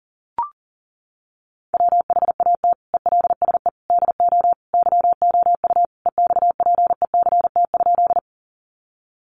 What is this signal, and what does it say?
Morse code, heard as a keyed tone, about 40 wpm, 700 Hz. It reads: WHAT ELSE DO YOU EXPECT?
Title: Sentences from Top 1100 Words 40wpm